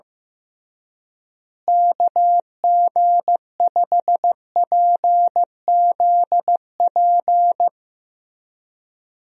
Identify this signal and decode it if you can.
Morse code, heard as a keyed tone, about 15 wpm, 700 Hz. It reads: KG5PZP
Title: USA Call Signs - 6 Characters - Repeated 3x - 15wpm